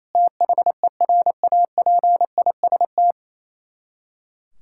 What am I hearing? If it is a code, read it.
Morse code, heard as a keyed tone, about 28 wpm, 700 Hz. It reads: THERAPIST
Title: K1USN SST QSO Practice 28wpm